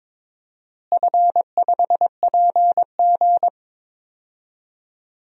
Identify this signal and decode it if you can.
Morse code, heard as a keyed tone, about 22 wpm, 700 Hz. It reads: F5PG
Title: DXCC-100 Call Signs - 4 Characters - Repeated 2x - 22wpm